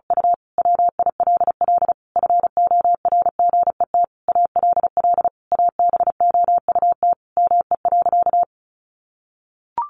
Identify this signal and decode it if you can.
Morse code, heard as a keyed tone, about 34 wpm, 700 Hz. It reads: U WILL FORGET ALL ABOUT ME.
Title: Sentences from Top 1200 Words 35wpm